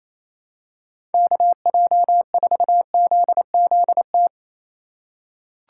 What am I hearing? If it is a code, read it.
Morse code, heard as a keyed tone, about 28 wpm, 700 Hz. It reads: KJ4ZZT
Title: USA Call Signs - 6 Characters - 4x Character Spacing - 28wpm